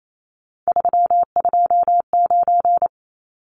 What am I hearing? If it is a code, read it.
Morse code, heard as a keyed tone, about 28 wpm, 700 Hz. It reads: 329
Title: Numbers - 3-Digit 28wpm